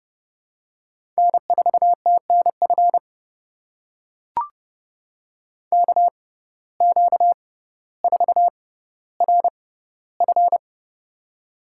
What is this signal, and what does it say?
Morse code, heard as a keyed tone, about 30 wpm, spaced as for 15 wpm, 700 Hz. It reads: N4TNF KQ4RF
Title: USA Call Signs - 5 Characters - 6x Character Spacing - 30wpm